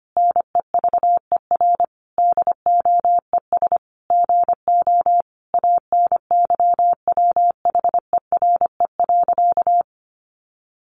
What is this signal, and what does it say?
Morse code, heard as a keyed tone, about 25 wpm, 700 Hz. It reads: NEVER DOES GO ANYWHERE.
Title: Sentences from Top 1400 Words 25wpm